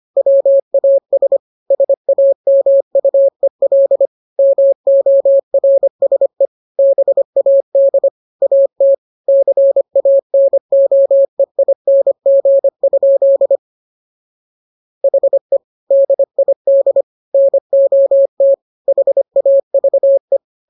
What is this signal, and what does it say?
Morse code, heard as a keyed tone, about 25 wpm, 550 Hz. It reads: WAS SAMUEL MORSE BAD AT CANOEING? HE DID NOT HAVE
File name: jokes / joke 1 25 0000